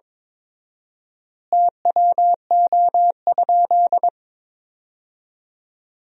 Morse code, heard as a keyed tone, about 21 wpm, 700 Hz. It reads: TWO?
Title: 4x Word-Spacing - Sentences from Top 100 Words - 22wpm